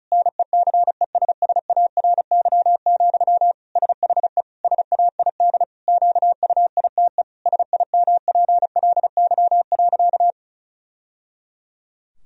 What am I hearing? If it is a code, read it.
Morse code, heard as a keyed tone, about 34 wpm, 700 Hz. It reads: NECESSARY, SHE SAID QUITE SIMPLY.
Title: Sentences from Top 1000 Words - Review Part 4 - 35wpm